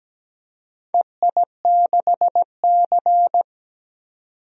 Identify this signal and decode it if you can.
Morse code, heard as a keyed tone, about 17 wpm, 700 Hz. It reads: EI6C
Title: DXCC-100 Call Signs - 4 Characters - 17wpm